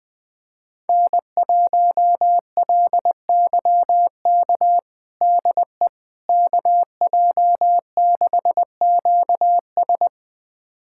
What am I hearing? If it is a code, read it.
Morse code, heard as a keyed tone, about 21 wpm, 700 Hz. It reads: N1LYK DE KJ6QS
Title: USA Callsign DE Callsign - 5 Characters - 20wpm